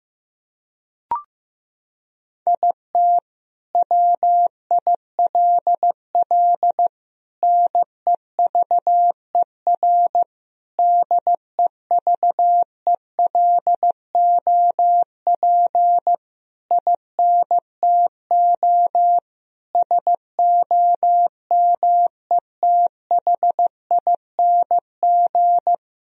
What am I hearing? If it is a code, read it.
Morse code, heard as a keyed tone, about 15 wpm, 700 Hz. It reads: IT WILL NEVER DEVELOP INTO SOMETHING
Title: Sentences from Top 1000 Words - Review Part 2 - 15wpm